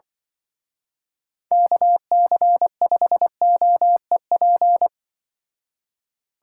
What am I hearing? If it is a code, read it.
Morse code, heard as a keyed tone, about 25 wpm, 700 Hz. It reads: KC5OEP
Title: Speed Racing - USA 2x3 Call Signs 20wpm